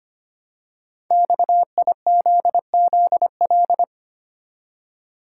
Morse code, heard as a keyed tone, about 25 wpm, 700 Hz. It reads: XIZZL
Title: Code Groups - 5 Letters - 25wpm